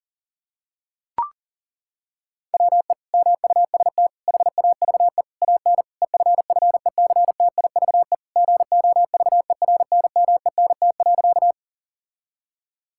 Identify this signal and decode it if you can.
Morse code, heard as a keyed tone, about 40 wpm, 690 Hz. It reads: WE MUST HAVE AN EFFECTIVE GOVERNMENT.
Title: Sentences from Top 1000 Words - Review Part 6 - 40wpm